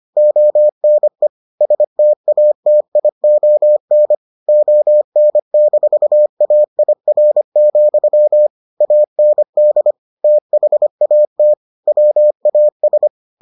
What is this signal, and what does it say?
Morse code, heard as a keyed tone, about 25 wpm, 600 Hz. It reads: ONE STATION ON-AIR, AND THAT WAS